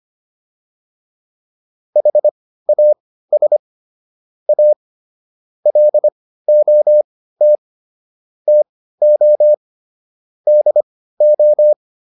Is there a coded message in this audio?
Yes: Morse code from 2.0 to 11.7 s, heard as a keyed tone, about 25 wpm, spaced as for 15 wpm, 600 Hz. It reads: HAS A LOT TO DO